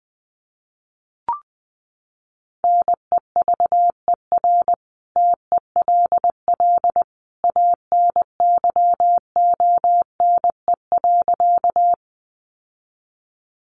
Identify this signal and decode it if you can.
Morse code, heard as a keyed tone, about 19 wpm, 700 Hz. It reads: NEVER TELL ANYONE.
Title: Sentences from Top 900 Words 20wpm